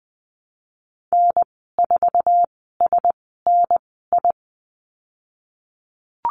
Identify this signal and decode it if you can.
Morse code, heard as a keyed tone, about 19 wpm, spaced as for 14 wpm, 700 Hz. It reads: N4SNI
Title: 055 Lesson 10g - Letter r - Call Signs 20wpm